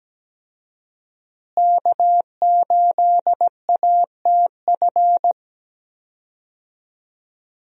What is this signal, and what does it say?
Morse code, heard as a keyed tone, about 17 wpm, 700 Hz. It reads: K8ATF
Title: USA Call Signs - 5 Characters - Repeated 2x - 17wpm